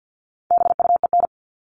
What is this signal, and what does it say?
Morse code, heard as a keyed tone, about 50 wpm, 700 Hz. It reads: BUEN
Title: All Cut Numbers - 4-Digit 50wpm